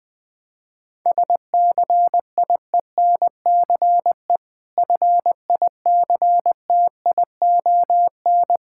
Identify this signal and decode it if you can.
Morse code, heard as a keyed tone, about 20 wpm, 690 Hz. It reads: SCIENCE FICTION
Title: Difficult Words with Hints - 20wpm